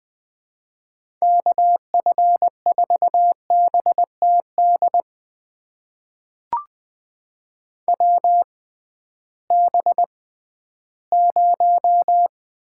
Morse code, heard as a keyed tone, about 20 wpm, spaced as for 13 wpm, 700 Hz. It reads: KF4BTD WB0
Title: USA Call Signs - 6 Characters - 6x Character Spacing - 20wpm